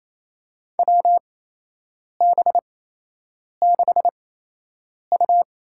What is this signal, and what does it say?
Morse code, heard as a keyed tone, about 28 wpm, spaced as for 8 wpm, 700 Hz. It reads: WB6U
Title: USA Call Signs - 6 Characters - 8x Character Spacing - 28wpm